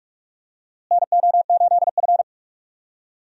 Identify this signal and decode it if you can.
Morse code, heard as a keyed tone, about 46 wpm, 700 Hz. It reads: NO8F
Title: USA Call Signs - 4 Characters - Repeated 3x - 45wpm